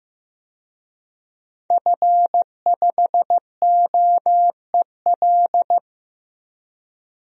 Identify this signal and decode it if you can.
Morse code, heard as a keyed tone, about 15 wpm, 700 Hz. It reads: F5OEL
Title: DXCC-100 Call Signs - 5 Characters - Repeated 2x - 15wpm